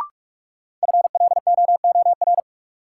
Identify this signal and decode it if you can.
Morse code, heard as a keyed tone, about 44 wpm, 700 Hz. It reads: FLOOR
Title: Sets of 3 Words - New Word Drill - 45wpm